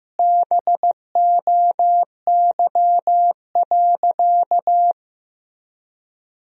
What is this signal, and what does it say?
Morse code, heard as a keyed tone, about 15 wpm, 700 Hz. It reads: BOY.
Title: 3x Word-Spacing - Sentences from Top 500 Words - 15wpm